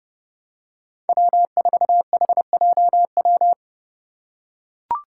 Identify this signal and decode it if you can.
Morse code, heard as a keyed tone, about 30 wpm, 700 Hz. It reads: W4HJW
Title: Five Challenging Letter-Number Combinations 30wpm